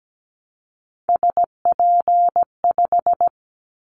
Morse code, heard as a keyed tone, about 17 wpm, 700 Hz. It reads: SP5
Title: Callsign Prefixes for Top 25 DXCC - 17wpm